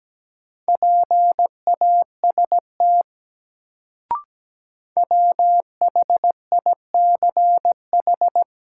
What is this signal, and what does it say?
Morse code, heard as a keyed tone, about 17 wpm, 700 Hz. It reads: PAST WHICH
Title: Sets of 2 Words - New Word Drill - 17wpm